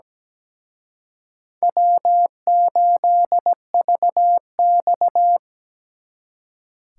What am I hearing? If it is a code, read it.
Morse code, heard as a keyed tone, about 20 wpm, 700 Hz. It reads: W8VX
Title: Speed Racing - USA 1x2 Call Signs 17wpm